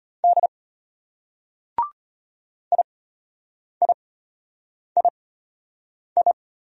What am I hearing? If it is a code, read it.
Morse code, heard as a keyed tone, about 33 wpm, spaced as for 8 wpm, 700 Hz. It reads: D IIII